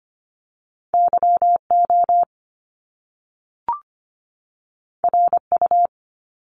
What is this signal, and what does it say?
Morse code, heard as a keyed tone, about 25 wpm, 700 Hz. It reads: YO RU
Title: Top Two Letter Combinations in Words 25wpm